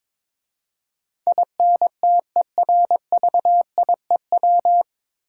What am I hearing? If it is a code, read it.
Morse code, heard as a keyed tone, about 22 wpm, 700 Hz. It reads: INTERVIEW
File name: 9 Letter Words - First 3 Letters Given - 22wpm